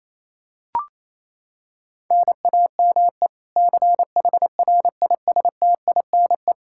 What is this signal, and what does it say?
Morse code, heard as a keyed tone, about 28 wpm, 700 Hz. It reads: NAME CHRISTINE
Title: Practice QSO - Exchange 1 - Beginning - 28wpm